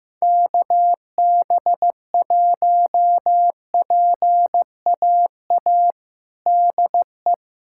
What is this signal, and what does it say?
Morse code, heard as a keyed tone, about 15 wpm, 700 Hz. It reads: KB1PAA DE
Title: USA Callsign DE Callsign - 6 Characters - 15wpm